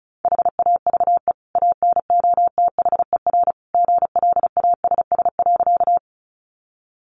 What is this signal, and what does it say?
Morse code, heard as a keyed tone, about 34 wpm, 700 Hz. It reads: HAVE ANOTHER GLASS.
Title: Sentences from Top 1300 Words 35wpm